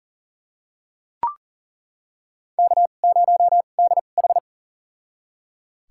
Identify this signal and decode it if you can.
Morse code, heard as a keyed tone, about 37 wpm, spaced as for 28 wpm, 700 Hz. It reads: K0DH
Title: USA Call Signs - 4 Characters - 2x Character Spacing - 40wpm